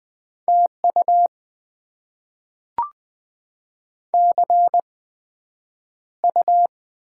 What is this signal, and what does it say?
Morse code, heard as a keyed tone, about 20 wpm, spaced as for 9 wpm, 700 Hz. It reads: TU CU
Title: Top 20 QSO Elements - 8x Character Spacing - 20wpm